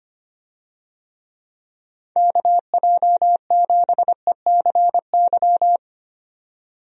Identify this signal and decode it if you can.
Morse code, heard as a keyed tone, about 25 wpm, 700 Hz. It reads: KJ7ECY
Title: USA Call Signs - 6 Characters - 25wpm